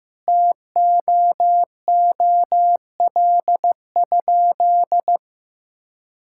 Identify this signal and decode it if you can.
Morse code, heard as a keyed tone, about 15 wpm, 700 Hz. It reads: TOOL?
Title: Sentences from Top 1000 Words - Review Part 6 - 15wpm